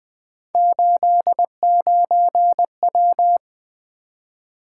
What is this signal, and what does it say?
Morse code, heard as a keyed tone, about 20 wpm, 700 Hz. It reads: 89W
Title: Three Letter-Number Combinations - 20wpm